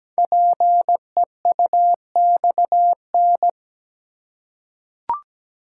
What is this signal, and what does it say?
Morse code, heard as a keyed tone, about 17 wpm, 700 Hz. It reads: PEUXN
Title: Code Groups - 5 Letters - 17wpm